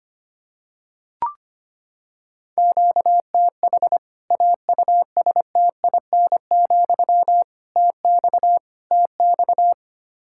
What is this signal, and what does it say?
Morse code, heard as a keyed tone, about 25 wpm, 700 Hz. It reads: QTH AUSTIN, TX TX
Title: K1USN SST QSO Practice 25wpm